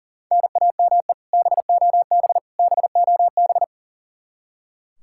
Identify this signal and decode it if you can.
Morse code, heard as a keyed tone, about 37 wpm, 700 Hz. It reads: NAME BOB BOB
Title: Practice QSO - Exchange 1 - Beginning - 40wpm